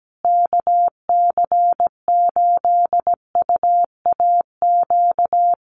Morse code, heard as a keyed tone, about 17 wpm, 700 Hz. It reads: KC8UAQ
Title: USA Call Signs Repeated 3x 17wpm